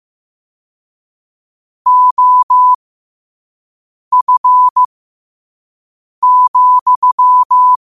Morse code, heard as a keyed tone, about 15 wpm, spaced as for 5 wpm, 1000 Hz. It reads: OF,
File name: qotdfiles / 5wpm